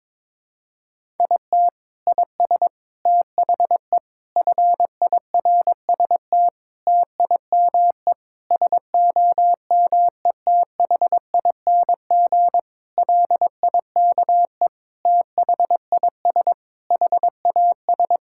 Morse code, heard as a keyed tone, about 22 wpm, 700 Hz. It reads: IT IS THE FIRST TIME SOMETHING LIKE THIS HAS
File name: practice / 22wpm / Sentences with 8-Word Phrases - 22wpm